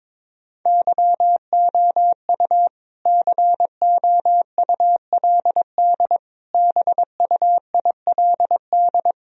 Morse code, heard as a keyed tone, about 21 wpm, 700 Hz. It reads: YOU COULD BUILD
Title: Sentences from Top 1000 Words - Review Part 2 - 22wpm